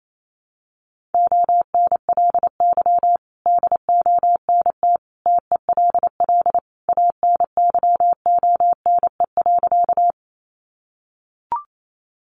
Morse code, heard as a keyed tone, about 28 wpm, 690 Hz. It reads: ONLY DONT TELL ANYONE.
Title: Sentences from Top 900 Words 28wpm